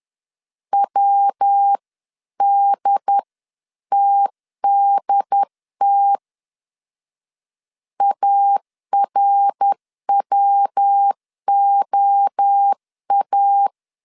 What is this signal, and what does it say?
Morse code, heard as a keyed tone, about 10 wpm, 800 Hz. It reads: WDTDT ARWOA
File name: 10wpm